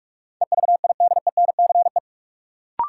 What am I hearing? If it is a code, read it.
Morse code, heard as a keyed tone, about 46 wpm, 700 Hz. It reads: EVIDENCE
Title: Sentences from Top 1300 Words - New Word Drill - 45wpm